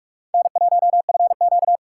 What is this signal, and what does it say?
Morse code, heard as a keyed tone, about 41 wpm, 700 Hz. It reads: N1FQ